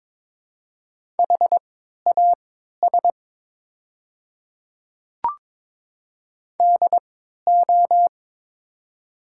Morse code, heard as a keyed tone, about 22 wpm, spaced as for 13 wpm, 700 Hz. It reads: HAS DO
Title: 3x Character Spacing - Top 200 Words - 22wpm